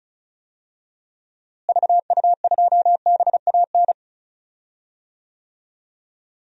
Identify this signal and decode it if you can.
Morse code, heard as a keyed tone, about 36 wpm, 700 Hz. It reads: VU2BAN
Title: DXCC-100 Call Signs - 6 Characters - Repeated 3x - 35wpm